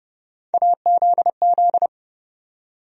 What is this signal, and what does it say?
Morse code, heard as a keyed tone, about 30 wpm, 700 Hz. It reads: AZZ
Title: USA Call Sign Postfixes - 3 Letters 30wpm